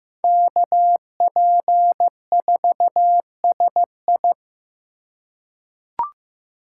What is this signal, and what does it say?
Morse code, heard as a keyed tone, about 15 wpm, 700 Hz. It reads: KP4SI
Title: USA Call Signs - 5 Characters - 2x Character Spacing - 15wpm